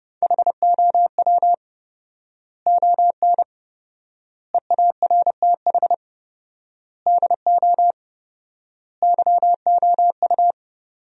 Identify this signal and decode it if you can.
Morse code, heard as a keyed tone, about 30 wpm, 700 Hz. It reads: HOW ON EARTH DO YOU
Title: 4x Word-Spacing - Sentences from Top 200 Words - 30wpm